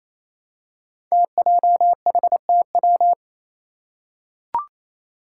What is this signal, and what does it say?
Morse code, heard as a keyed tone, about 28 wpm, 700 Hz. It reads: TJHTW
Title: Five Challenging Letter-Number Combinations 28wpm